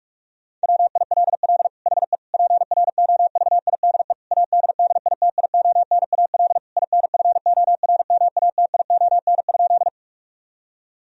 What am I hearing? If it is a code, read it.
Morse code, heard as a keyed tone, about 46 wpm, 700 Hz. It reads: WILL HE PROVIDE ADDITIONAL INFORMATION?
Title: Sentences from Top 1000 Words - Review Part 4 - 45wpm